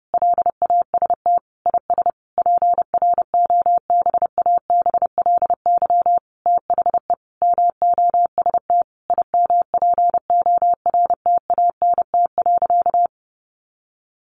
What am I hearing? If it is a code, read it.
Morse code, heard as a keyed tone, about 30 wpm, 700 Hz. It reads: LAST IS PROBABLY THE MOST IMPORTANT.
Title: Sentences from Top 700 Words 30wpm